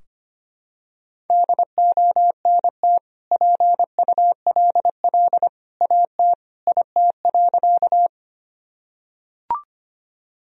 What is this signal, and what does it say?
Morse code, heard as a keyed tone, about 25 wpm, 700 Hz. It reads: DONT PULL AT IT.
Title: Sentences from Top 1000 Words - Review Part 2 - 25wpm